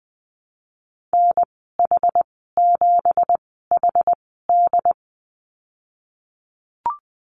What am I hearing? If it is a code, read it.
Morse code, heard as a keyed tone, about 20 wpm, spaced as for 15 wpm, 700 Hz. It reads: NH7HD
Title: 231 Lesson 37g - Letter x - Call Signs 20wpm